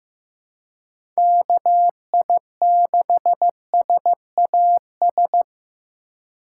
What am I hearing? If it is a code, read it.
Morse code, heard as a keyed tone, about 15 wpm, 700 Hz. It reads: KI6SAS
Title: USA Call Signs - 2x Character Spacing - 15wpm